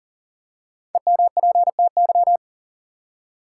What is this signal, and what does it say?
Morse code, heard as a keyed tone, about 40 wpm, 700 Hz. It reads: EMPTY
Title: Five Letter Words - 40wpm